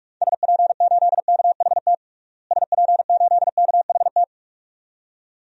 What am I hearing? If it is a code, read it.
Morse code, heard as a keyed tone, about 47 wpm, 700 Hz. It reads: SP8KHT SP8KHT
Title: DXCC-100 Call Signs - 6 Characters - Repeated 2x - 45wpm